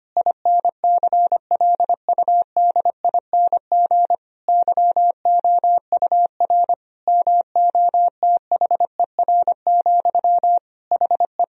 Morse code, heard as a keyed tone, about 25 wpm, 700 Hz. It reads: INCLUDING YOUR MOTHER, HE
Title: Sentences from Top 1000 Words - Review Part 4 - 25wpm